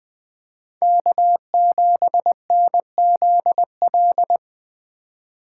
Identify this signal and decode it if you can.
Morse code, heard as a keyed tone, about 20 wpm, 700 Hz. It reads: K7NZL
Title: Speed Racing - USA Call Signs 17wpm